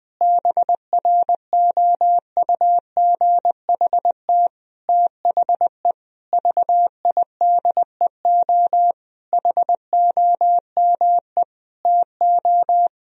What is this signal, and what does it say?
Morse code, heard as a keyed tone, about 20 wpm, 700 Hz. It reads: BROUGHT THE VIDEO HOME TO
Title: Sentences from Top 1000 Words - Review Part 4 - 20wpm